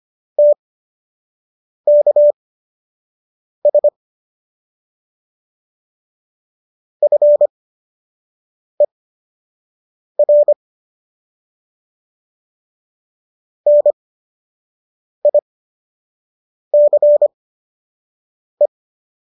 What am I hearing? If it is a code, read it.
Morse code, heard as a keyed tone, about 25 wpm, spaced as for 6 wpm, 590 Hz. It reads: TKS FER NICE